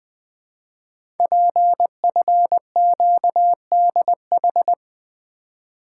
Morse code, heard as a keyed tone, about 20 wpm, 700 Hz. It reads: PFQDH
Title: Code Groups - 5 Letters - 20wpm